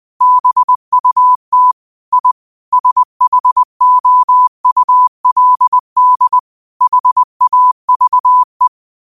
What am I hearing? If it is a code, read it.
Morse code, heard as a keyed tone, about 20 wpm, 1000 Hz. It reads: BUT I SHOULD HAVE
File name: 20wpm